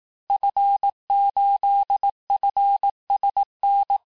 Morse code, heard as a keyed tone, about 18 wpm, 800 Hz. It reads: F8FSN
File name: cours-du-lundi files / cours-1594--FIN-de-QSO-vitesse-17mpm